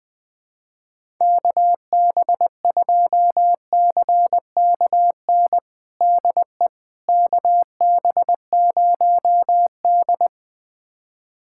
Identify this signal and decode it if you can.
Morse code, heard as a keyed tone, about 20 wpm, 700 Hz. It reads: KB2CKN DE KB0D
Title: USA Callsign DE Callsign 20wpm